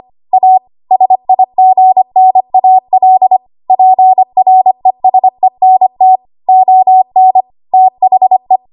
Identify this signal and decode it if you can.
Morse code, heard as a keyed tone, about 25 wpm, 750 Hz. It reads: A SIGNAL PRESENT ON THE
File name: Archive / 25 WPM / 25WPM